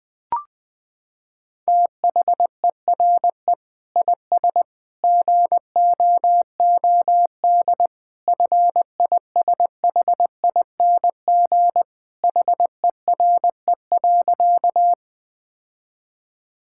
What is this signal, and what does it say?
Morse code, heard as a keyed tone, about 19 wpm, 700 Hz. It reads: THERE IS GOOD FISHING HERE.
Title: Sentences from Top 1000 Words - Review Part 5 - 20wpm